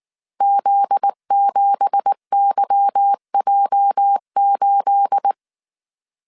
Morse code, heard as a keyed tone, about 19 wpm, 800 Hz. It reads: Z7YJ8
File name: CW / DET / 1 9 1500C S A 18wpm 2459